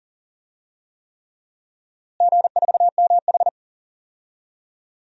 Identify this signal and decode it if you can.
Morse code, heard as a keyed tone, about 40 wpm, 700 Hz. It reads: G4MH
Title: DXCC-100 Call Signs - 4 Characters - Repeated 3x - 40wpm